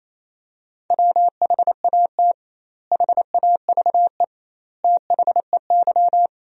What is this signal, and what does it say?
Morse code, heard as a keyed tone, about 28 wpm, 700 Hz. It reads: WHAT HAVE THEY
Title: 2x Word-Spacing - Sentences from Top 500 Words - 28wpm